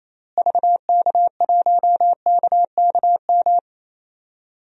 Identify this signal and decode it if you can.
Morse code, heard as a keyed tone, about 28 wpm, 700 Hz. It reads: VK1KKM
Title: DXCC-100 Call Signs - 6 Characters - Repeated 3x - 28wpm